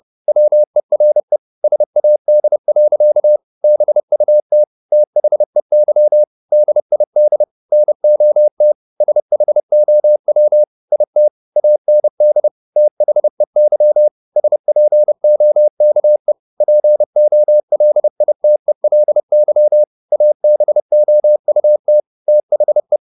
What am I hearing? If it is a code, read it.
Morse code, heard as a keyed tone, about 30 wpm, 600 Hz. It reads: WERE SAD. BUT THEY DID NOT SHOW IT AND THEY SPOKE POLITELY ABOUT THE
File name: the old man and the sea / the old man and the sea 1 30wpm